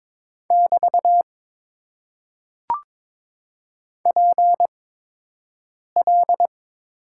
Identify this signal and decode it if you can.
Morse code, heard as a keyed tone, about 22 wpm, spaced as for 6 wpm, 700 Hz. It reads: <BT> PL
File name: Top 90 QSO Elements - 8x Character Spacing - 22wpm